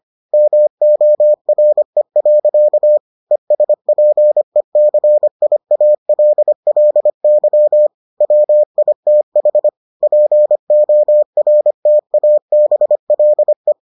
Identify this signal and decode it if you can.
Morse code, heard as a keyed tone, about 25 wpm, 600 Hz. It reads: MORE. ESPECIALLY WITH PORTABLE